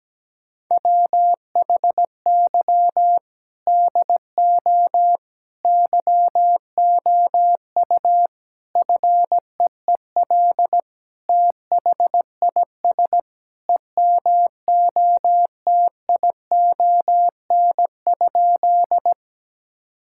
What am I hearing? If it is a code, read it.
Morse code, heard as a keyed tone, about 17 wpm, 700 Hz. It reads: WHY DO YOU FEEL THIS EMOTION?
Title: Sentences from Top 1600 Words 17wpm